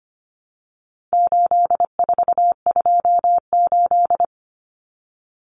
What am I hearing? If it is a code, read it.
Morse code, heard as a keyed tone, about 25 wpm, 700 Hz. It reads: 8428
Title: Numbers - 4-Digit 25wpm